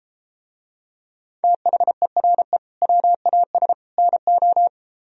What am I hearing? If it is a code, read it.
Morse code, heard as a keyed tone, about 33 wpm, 700 Hz. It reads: THERE WAS NO